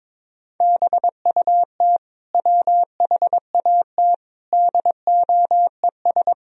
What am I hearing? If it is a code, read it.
Morse code, heard as a keyed tone, about 22 wpm, 700 Hz. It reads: BUT WHAT DOES
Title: Sentences from Top 1000 Words - Review Part 4 - 22wpm